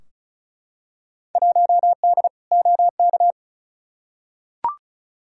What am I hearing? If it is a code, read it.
Morse code, heard as a keyed tone, about 36 wpm, 700 Hz. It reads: 1D OK
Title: Field Day Exchanges - 35wpm